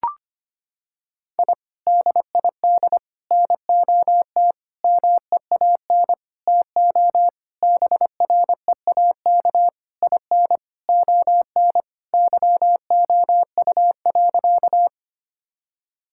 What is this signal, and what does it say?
Morse code, heard as a keyed tone, about 25 wpm, 700 Hz. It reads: I DID NOT MEAN TO BREAK IN ON YOU.
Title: Sentences from Top 1000 Words - Review Part 6 - 25wpm